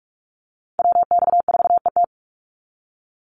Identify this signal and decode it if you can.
Morse code, heard as a keyed tone, about 44 wpm, 700 Hz. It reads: WX4ET
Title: NA SOTA Chaser Callsigns 45wpm